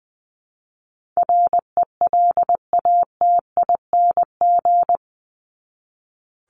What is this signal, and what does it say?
Morse code, heard as a keyed tone, about 20 wpm, 700 Hz. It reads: RELATING
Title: ING Suffix - Top 100 Words with Base Words 20wpm